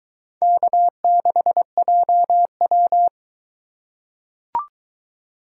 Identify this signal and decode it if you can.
Morse code, heard as a keyed tone, about 23 wpm, 700 Hz. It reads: K6JW